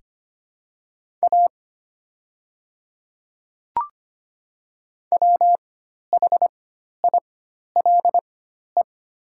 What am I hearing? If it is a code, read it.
Morse code, heard as a keyed tone, about 25 wpm, spaced as for 12 wpm, 700 Hz. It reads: A WHILE
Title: 4x Character Spacing - Top 300 Words - 25wpm